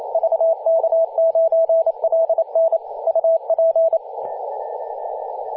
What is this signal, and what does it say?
Morse code, heard as a keyed tone, about 28 wpm, 660 Hz, with radio noise. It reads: VK9LN UP
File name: vk9ln-14-005mhz-cw